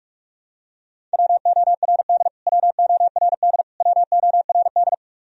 Morse code, heard as a keyed tone, about 46 wpm, 700 Hz. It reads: WORD WORD WORD
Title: Top 100 Words Repeated 45wpm